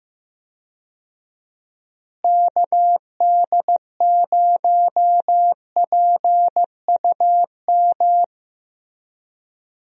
Morse code, heard as a keyed tone, about 15 wpm, 700 Hz. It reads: KD0PUM
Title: USA Call Signs - 6 Characters - Repeated 3x - 15wpm